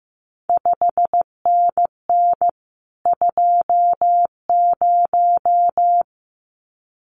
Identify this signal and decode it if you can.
Morse code, heard as a keyed tone, about 15 wpm, 700 Hz. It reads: 5NN 20
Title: CQ WW DX Contest Exchange - 15wpm